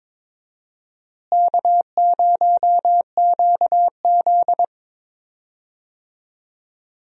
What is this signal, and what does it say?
Morse code, heard as a keyed tone, about 22 wpm, 700 Hz. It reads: K0QZ
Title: USA Call Signs - 4 Characters - Repeated 3x - 22wpm